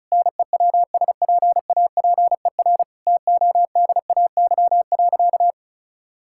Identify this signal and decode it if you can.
Morse code, heard as a keyed tone, about 34 wpm, 700 Hz. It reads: NEWSPAPER TODAY.
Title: Sentences from Top 1000 Words - Review Part 6 - 35wpm